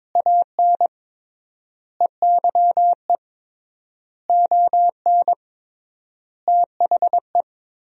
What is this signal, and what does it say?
Morse code, heard as a keyed tone, about 22 wpm, 700 Hz. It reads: AN EYE ON THE
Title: 3x Word-Spacing - Sentences from Top 300 Words - 22wpm